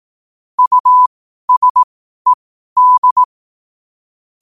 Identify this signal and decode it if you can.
Morse code, heard as a keyed tone, about 18 wpm, spaced as for 12 wpm, 1000 Hz. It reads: USED
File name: qotdfiles / 12wpm